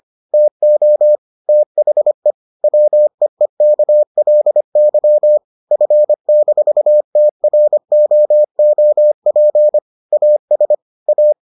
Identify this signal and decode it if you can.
Morse code, heard as a keyed tone, about 25 wpm, 600 Hz. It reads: TO THE WEEKLY F-TROOP AS A